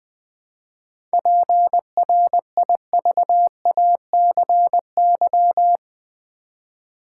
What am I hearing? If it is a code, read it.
Morse code, heard as a keyed tone, about 20 wpm, 700 Hz. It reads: PRIVACY
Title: 7 Letter Words - First 3 Letters Given - 20wpm